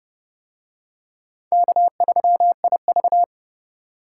Morse code, heard as a keyed tone, about 29 wpm, 700 Hz. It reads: K3IV